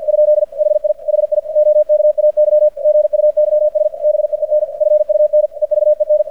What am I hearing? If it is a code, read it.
Morse code, heard as a keyed tone, about 50 wpm, 600 Hz, with radio noise. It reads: ? HI HI OKAY GUYS BEEN FUN EVEN
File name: archived / CRSnet